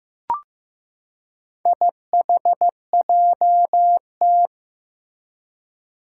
Morse code, heard as a keyed tone, about 15 wpm, 700 Hz. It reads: IHJT
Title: Four Challenging Letter-Number Combinations 15wpm